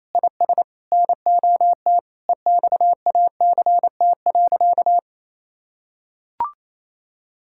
Morse code, heard as a keyed tone, about 28 wpm, 700 Hz. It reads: IS NOT EXACT.
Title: Sentences from Top 1600 Words 28wpm